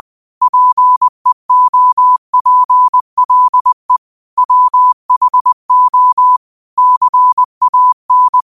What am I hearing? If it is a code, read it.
Morse code, heard as a keyed tone, about 20 wpm, 1000 Hz. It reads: PEOPLE WHO CAN